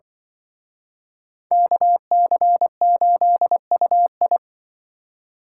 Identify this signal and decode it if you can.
Morse code, heard as a keyed tone, about 24 wpm, 700 Hz. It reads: KC8UI
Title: Speed Racing - USA 2x2 Call Signs 20wpm